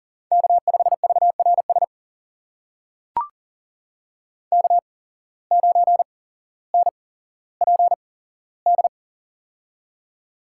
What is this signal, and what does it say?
Morse code, heard as a keyed tone, about 37 wpm, spaced as for 17 wpm, 700 Hz. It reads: K5VRS K9NPD
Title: USA Call Signs - 5 Characters - 8x Character Spacing - 40wpm